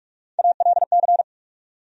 Morse code, heard as a keyed tone, about 44 wpm, 700 Hz. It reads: ALC
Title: USA Call Sign Postfixes - 3 Letters 45wpm